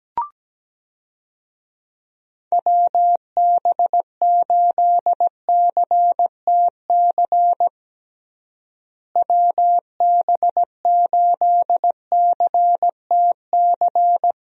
Morse code, heard as a keyed tone, about 17 wpm, 700 Hz. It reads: WB8CTC WB8CTC
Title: USA Call Signs - 6 Characters - Repeated 3x - 17wpm